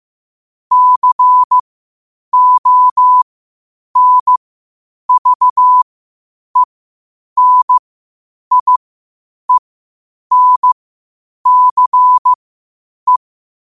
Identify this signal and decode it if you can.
Morse code, heard as a keyed tone, about 15 wpm, spaced as for 8 wpm, 1000 Hz. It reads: CONVENIENCE
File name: qotdfiles / 8wpm